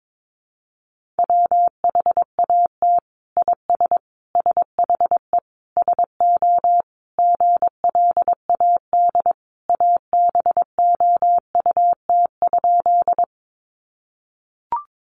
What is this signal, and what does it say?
Morse code, heard as a keyed tone, about 22 wpm, 700 Hz. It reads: WHAT IS SHE SO GLAD ABOUT?
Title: Sentences from Top 800 Words 22wpm